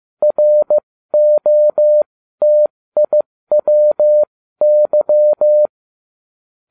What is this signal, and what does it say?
Morse code, heard as a keyed tone, about 15 wpm, spaced as for 13 wpm, 600 Hz. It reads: ROTIWY